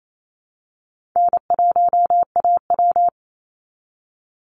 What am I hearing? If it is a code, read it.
Morse code, heard as a keyed tone, about 28 wpm, 700 Hz. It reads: N1AW
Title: NA SOTA Chaser Callsigns 28wpm